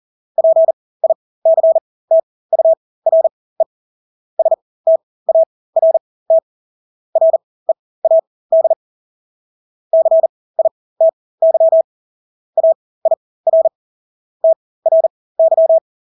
Morse code, heard as a keyed tone, about 40 wpm, spaced as for 20 wpm, 650 Hz. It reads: PICTURE START READ CITY AIR TRY
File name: Engelske ord 40-20 wpm